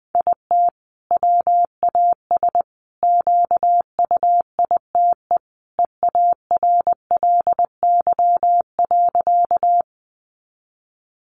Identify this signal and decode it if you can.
Morse code, heard as a keyed tone, about 20 wpm, 700 Hz. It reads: IT WAS QUITE EARLY.
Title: Sentences from Top 700 Words 20wpm